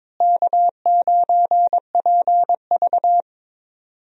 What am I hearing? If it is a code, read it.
Morse code, heard as a keyed tone, about 22 wpm, 700 Hz. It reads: K9PV
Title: USA Call Signs - 4 Characters - Repeated 2x - 22wpm